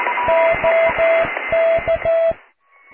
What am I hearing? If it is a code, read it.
Morse code, heard as a keyed tone, about 14 wpm, 660 Hz, with radio noise. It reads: OK
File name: cw / QSO+QRM